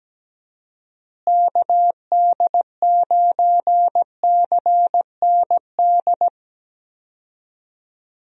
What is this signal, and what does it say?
Morse code, heard as a keyed tone, about 17 wpm, 700 Hz. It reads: KD9CND
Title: USA Call Signs - 6 Characters - Repeated 2x - 17wpm